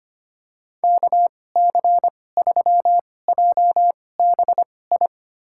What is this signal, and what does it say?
Morse code, heard as a keyed tone, about 25 wpm, spaced as for 18 wpm, 700 Hz. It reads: KC3JBI
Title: USA Call Signs - 6 Characters - 2x Character Spacing - 25wpm